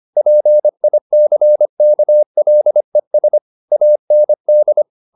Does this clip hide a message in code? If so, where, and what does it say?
Yes: Morse code from 0.2 to 4.8 s, heard as a keyed tone, about 25 wpm, 600 Hz. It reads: PICKLES AND